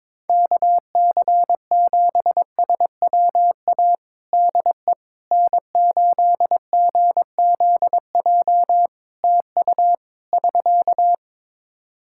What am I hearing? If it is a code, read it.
Morse code, heard as a keyed tone, about 22 wpm, 700 Hz. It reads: KC7SWA DE N8GZJ TU <SK>
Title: Practice QSO - Exchange 4 - Ending - 22wpm